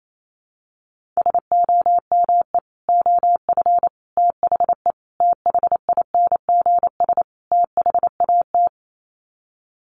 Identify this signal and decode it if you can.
Morse code, heard as a keyed tone, about 28 wpm, 700 Hz. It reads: SOME OF THE THINGS THAT
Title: Sets of 5 Words 28wpm